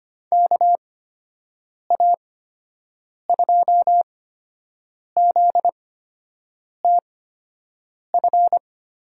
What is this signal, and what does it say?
Morse code, heard as a keyed tone, about 25 wpm, spaced as for 7 wpm, 700 Hz. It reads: KA2ZTF
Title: USA Call Signs - 6 Characters - 8x Character Spacing - 25wpm